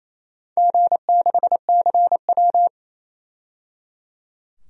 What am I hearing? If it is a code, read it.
Morse code, heard as a keyed tone, about 28 wpm, 700 Hz. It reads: G6CW
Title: DXCC-100 Call Signs - 4 Characters - 28wpm